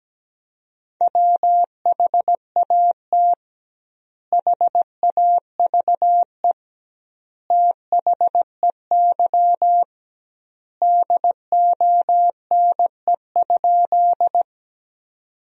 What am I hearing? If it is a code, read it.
Morse code, heard as a keyed tone, about 17 wpm, 700 Hz. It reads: WHAT HAVE THEY DONE?
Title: 2x Word-Spacing - Sentences from Top 500 Words - 17wpm